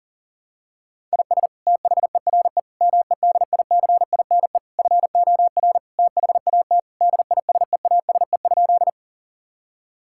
Morse code, heard as a keyed tone, about 40 wpm, 700 Hz. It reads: IS THERE MEDICINE FOR THAT DISEASE?
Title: Sentences from Top 1000 Words - Review Part 6 - 40wpm